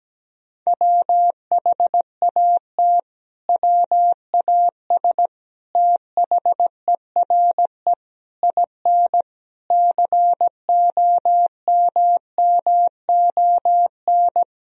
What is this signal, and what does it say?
Morse code, heard as a keyed tone, about 17 wpm, 700 Hz. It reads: WHAT WAS THERE IN COMMON
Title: Sentences from Top 1000 Words - Review Part 3 - 17wpm